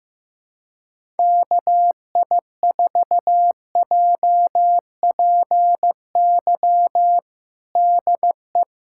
Morse code, heard as a keyed tone, about 15 wpm, 700 Hz. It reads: KI4JPY DE
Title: USA Callsign DE Callsign 15wpm